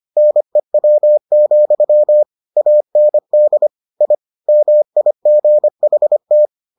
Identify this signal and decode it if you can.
Morse code, heard as a keyed tone, about 25 wpm, 600 Hz. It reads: NEW, AND I MIGHT